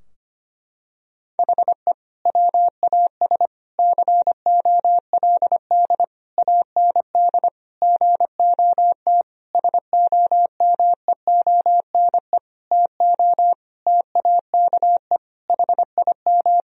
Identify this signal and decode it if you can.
Morse code, heard as a keyed tone, about 25 wpm, 700 Hz. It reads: HE WAS COLD AND GOT SOMEONE TO TAKE HIM
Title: Sentences from Top 1000 Words - Review Part 4 - 25wpm